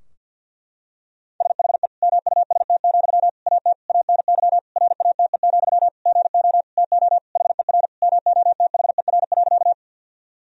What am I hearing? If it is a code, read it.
Morse code, heard as a keyed tone, about 50 wpm, 700 Hz. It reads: SHE MUST, AT ANY RATE, GO TO HER MOTHER.
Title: Sentences from Top 1000 Words - Review Part 4 - 50wpm